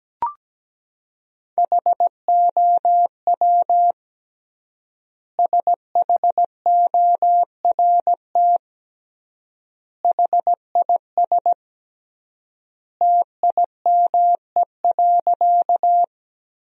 Morse code, heard as a keyed tone, about 17 wpm, 700 Hz. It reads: HOW SHORT HIS TIME.
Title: 3x Word-Spacing - Sentences from Top 400 Words - 17wpm